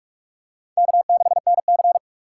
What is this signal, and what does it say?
Morse code, heard as a keyed tone, about 44 wpm, 700 Hz. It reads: K6NC